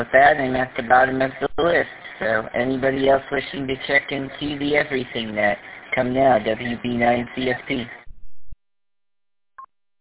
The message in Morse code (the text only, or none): none